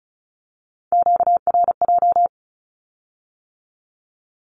Morse code, heard as a keyed tone, about 36 wpm, 700 Hz. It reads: QRJ
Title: Q-Codes with Spoken Meaning - 35wpm